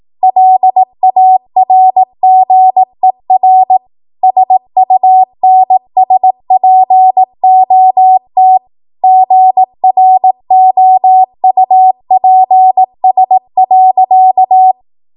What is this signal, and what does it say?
Morse code, heard as a keyed tone, about 18 wpm, 750 Hz. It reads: LARGER SUNSPOT GROUPS.
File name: Morse / 18WPM